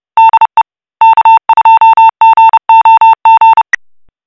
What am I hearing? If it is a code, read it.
Morse code, heard as a keyed tone, about 30 wpm, 900 Hz. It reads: DE K2GOG
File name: e9e61cbe-6188-4f30-833c / SSTV OVER FM IMAGE THREE UNDR NET